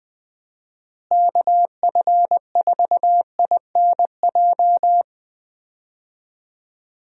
Morse code, heard as a keyed tone, about 20 wpm, 700 Hz. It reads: KF4INJ
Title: USA Call Signs - 6 Characters - Repeated 3x - 20wpm